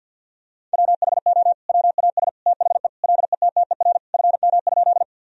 Morse code, heard as a keyed tone, about 46 wpm, 700 Hz. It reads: WHO WAS THE LETTER FM?
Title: Sentences from Top 300 Words and Abbreviations 50wpm